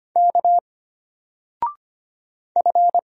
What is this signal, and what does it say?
Morse code, heard as a keyed tone, about 25 wpm, 700 Hz. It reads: K F
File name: Single-Letter-Number-25wpm